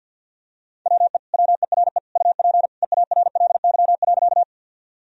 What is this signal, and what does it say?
Morse code, heard as a keyed tone, about 50 wpm, 700 Hz. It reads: WE WERE UP EARLY.
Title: Sentences from Top 1000 Words - Review Part 2 - 50wpm